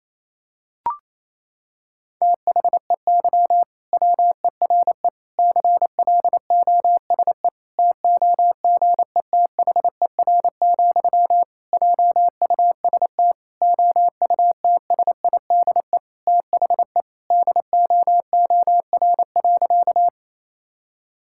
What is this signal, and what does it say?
Morse code, heard as a keyed tone, about 27 wpm, 700 Hz. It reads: THEY WERE CLOSE TOGETHER, JUST OUTSIDE THE DOOR.
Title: Sentences from Top 1000 Words - Review Part 6 - 28wpm